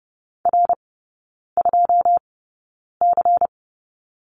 Morse code, heard as a keyed tone, about 30 wpm, spaced as for 9 wpm, 700 Hz. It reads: R2C
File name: Lesson 16e - Letter c - Three Characters 30wpm